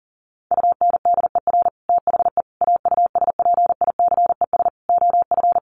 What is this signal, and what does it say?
Morse code, heard as a keyed tone, about 40 wpm, 700 Hz. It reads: UNDER THE AUSPICES OF
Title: Sets of 5 Words Encore2 40wpm